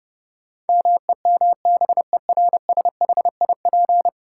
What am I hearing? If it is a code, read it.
Morse code, heard as a keyed tone, about 32 wpm, 700 Hz. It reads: MEMBERSHIP
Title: Sentences from Top 1400 Words - New Word Drill Alt - 30wpm